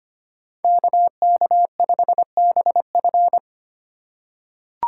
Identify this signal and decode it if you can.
Morse code, heard as a keyed tone, about 25 wpm, 700 Hz. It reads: KK5BF
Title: USA Call Signs - 5 Characters - 4x Character Spacing - 25wpm